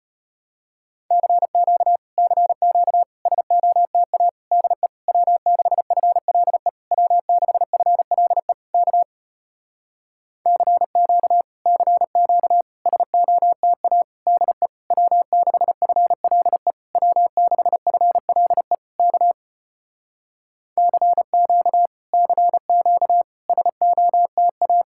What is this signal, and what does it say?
Morse code, heard as a keyed tone, about 33 wpm, 700 Hz. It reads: CQ CQ SOTA DE W6FLE W6FLE K CQ CQ SOTA DE W6FLE W6FLE K CQ CQ SOTA